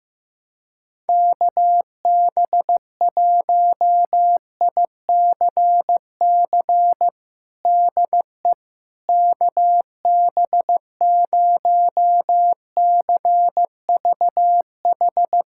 Morse code, heard as a keyed tone, about 15 wpm, 700 Hz. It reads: KB1ICC DE KB0CVH
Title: USA Callsign DE Callsign 15wpm